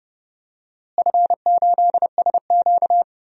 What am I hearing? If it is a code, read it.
Morse code, heard as a keyed tone, about 30 wpm, 700 Hz. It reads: F8SQ
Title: DXCC-100 Call Signs - 4 Characters - Repeated 2x - 30wpm